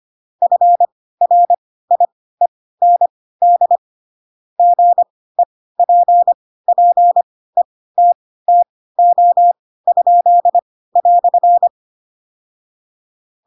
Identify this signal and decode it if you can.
Morse code, heard as a keyed tone, about 25 wpm, spaced as for 16 wpm, 700 Hz. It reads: FRIEND GEPPETTO?"
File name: Pinocchio / Pinocchio ch2 10 25 ik1hge